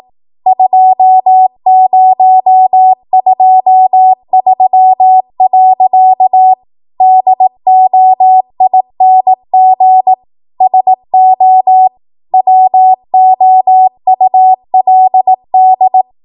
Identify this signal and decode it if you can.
Morse code, heard as a keyed tone, about 18 wpm, 750 Hz. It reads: 2023. DOING SO WOULD
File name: Morse / 18WPM